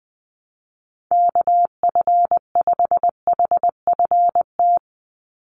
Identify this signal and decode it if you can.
Morse code, heard as a keyed tone, about 20 wpm, 700 Hz. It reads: KF5HFT
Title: USA Call Signs 20wpm